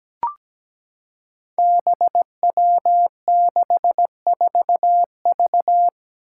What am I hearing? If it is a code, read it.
Morse code, heard as a keyed tone, about 17 wpm, 700 Hz. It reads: BW64V
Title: Five Challenging Letter-Number Combinations 17wpm